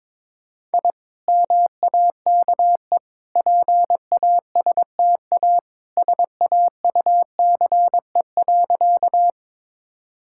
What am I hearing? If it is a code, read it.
Morse code, heard as a keyed tone, about 22 wpm, 700 Hz. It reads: I MAKE PASTA SAUCE.
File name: practice / 22wpm / Easy Sentences with Hints - 22wpm